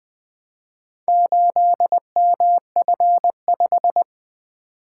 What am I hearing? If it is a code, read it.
Morse code, heard as a keyed tone, about 20 wpm, 700 Hz. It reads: 8MF5
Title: Four Letter-Number Combinations - 20wpm